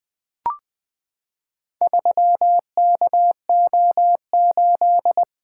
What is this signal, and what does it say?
Morse code, heard as a keyed tone, about 20 wpm, 700 Hz. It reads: 3KO8
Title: Four Letter-Number Combinations - 20wpm